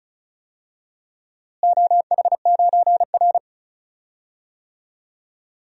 Morse code, heard as a keyed tone, about 36 wpm, 700 Hz. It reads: OH9R
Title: DXCC-100 Call Signs - 4 Characters - Repeated 3x - 35wpm